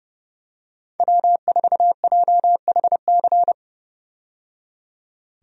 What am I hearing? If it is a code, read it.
Morse code, heard as a keyed tone, about 30 wpm, 700 Hz. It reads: W4JHC
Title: USA Call Signs - 5 Characters - 30wpm